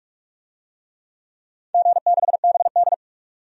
Morse code, heard as a keyed tone, about 44 wpm, 700 Hz. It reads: G6BD
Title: DXCC-100 Call Signs - 4 Characters - 45wpm